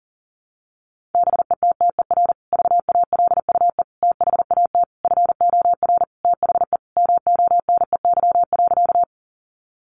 Morse code, heard as a keyed tone, about 40 wpm, 700 Hz. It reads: BETTER VALUE THAT FOR THE MONEY.
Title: Sentences from Top 600 Words 40wpm